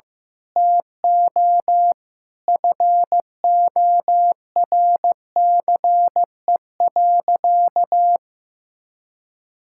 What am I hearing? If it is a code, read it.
Morse code, heard as a keyed tone, about 15 wpm, 700 Hz. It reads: TO FORCE.
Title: Sentences from Top 1000 Words - Review Part 6 - 15wpm